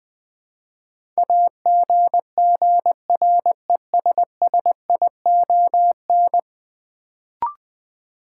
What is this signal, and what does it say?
Morse code, heard as a keyed tone, about 20 wpm, 700 Hz. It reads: AGGRESSION
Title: Build A Word - 5 Letters At A Time - 10-Letter Words - 20wpm